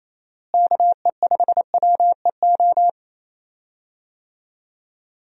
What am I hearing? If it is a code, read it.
Morse code, heard as a keyed tone, about 28 wpm, 700 Hz. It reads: KE5WEO
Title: USA Call Signs - 6 Characters - Repeated 2x - 28wpm